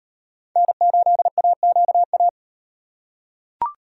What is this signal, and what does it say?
Morse code, heard as a keyed tone, about 38 wpm, 700 Hz. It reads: N8AQA